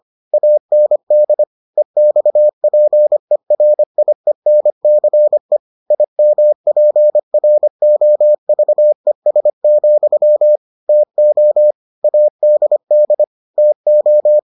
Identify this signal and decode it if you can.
Morse code, heard as a keyed tone, about 25 wpm, 600 Hz. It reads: AND EXPERIENCE IMPROVES, TO ADD TO